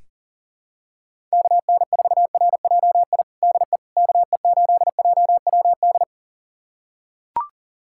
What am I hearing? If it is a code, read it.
Morse code, heard as a keyed tone, about 40 wpm, 700 Hz. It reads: KN4RJI DE KE8JWD
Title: USA Callsign DE Callsign - 6 Characters - 40wpm